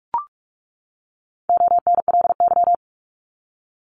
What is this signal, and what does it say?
Morse code, heard as a keyed tone, about 44 wpm, 700 Hz. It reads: ONLY
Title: Top 200 Words 45wpm